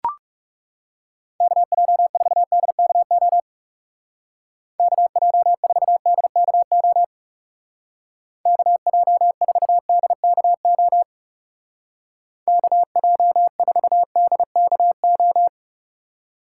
Morse code, heard as a keyed tone, about 36 wpm, 700 Hz. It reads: KJ4DKO KJ4DKO KJ4DKO KJ4DKO
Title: Speed Racing - USA 2x3 Call Signs 30wpm